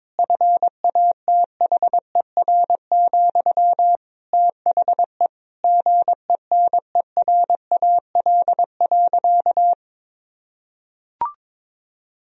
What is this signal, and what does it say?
Morse code, heard as a keyed tone, about 22 wpm, 700 Hz. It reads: FATHER, THE GENERAL.
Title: Sentences from Top 1000 Words - Review Part 3 - 22wpm